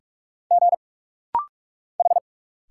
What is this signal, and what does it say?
Morse code, heard as a keyed tone, about 46 wpm, 700 Hz. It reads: G H
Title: Single Letters - Rapid-Fire 45wpm